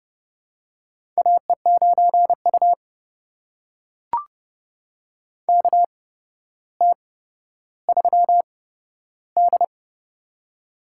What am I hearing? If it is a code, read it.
Morse code, heard as a keyed tone, about 30 wpm, spaced as for 13 wpm, 700 Hz. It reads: AE9U KT3D
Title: USA Call Signs - 4 Characters - 8x Character Spacing - 30wpm